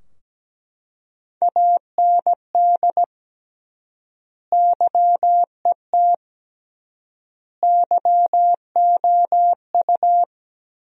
Morse code, heard as a keyed tone, about 17 wpm, 700 Hz. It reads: AND YET YOU
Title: 3x Word-Spacing - Sentences from Top 500 Words - 17wpm